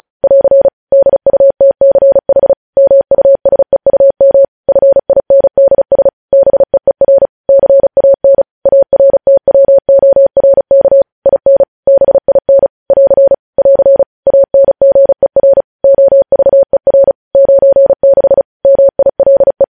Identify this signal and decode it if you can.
Morse code, heard as a keyed tone, about 36 wpm, 550 Hz. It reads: <AR> DUTCH MUSEUM FINDS BEER CAN ARTWORK IN BIN <AR> <AR> ANGER OVER 96 MILE